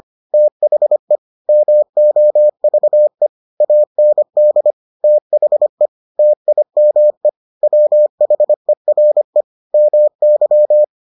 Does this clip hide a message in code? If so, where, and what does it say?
Yes: Morse code from 0.3 to 10.8 s, heard as a keyed tone, about 25 wpm, 600 Hz. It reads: THE MOVE AND THE TIME WHERE MY